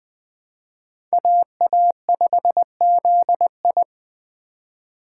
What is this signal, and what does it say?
Morse code, heard as a keyed tone, about 20 wpm, 700 Hz. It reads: AA5ZI
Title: Speed Racing - USA 2x2 Call Signs 17wpm